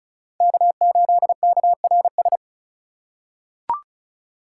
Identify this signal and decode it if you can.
Morse code, heard as a keyed tone, about 34 wpm, 700 Hz. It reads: K8KRS
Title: USA Call Signs - 5 Characters - 3x Character Spacing - 35wpm